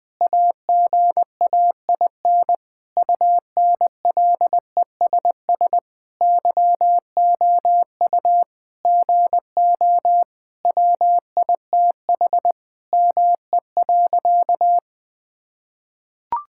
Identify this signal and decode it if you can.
Morse code, heard as a keyed tone, about 20 wpm, 700 Hz. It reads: AGAIN UNLESS YOU GO WITH ME.
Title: Sentences from Top 1000 Words - Review Part 6 - 20wpm